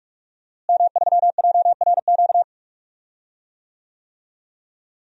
Morse code, heard as a keyed tone, about 46 wpm, 700 Hz. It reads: M3JRQ
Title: DXCC-100 Call Signs - 5 Characters - Repeated 2x - 45wpm